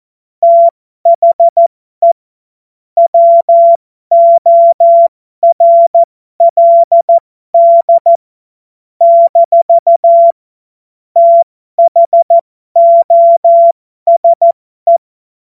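Morse code, heard as a keyed tone, about 14 wpm, spaced as for 12 wpm, 680 Hz. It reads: THE WORLD - THOSE
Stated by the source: Morse code joke practice.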